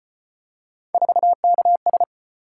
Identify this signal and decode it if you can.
Morse code, heard as a keyed tone, about 33 wpm, 700 Hz. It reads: 4KS